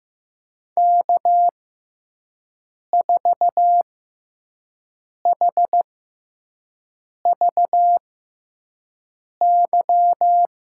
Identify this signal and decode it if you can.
Morse code, heard as a keyed tone, about 15 wpm, spaced as for 5 wpm, 700 Hz. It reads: K4HVY
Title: USA Call Signs - 5 Characters - 6x Character Spacing - 15wpm